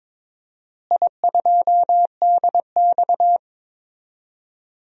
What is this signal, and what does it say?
Morse code, heard as a keyed tone, about 22 wpm, 700 Hz. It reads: I2DX
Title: DXCC-100 Call Signs - 4 Characters - Repeated 2x - 22wpm